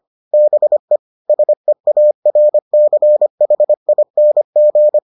Morse code, heard as a keyed tone, about 25 wpm, 600 Hz. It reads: BE SEARCHING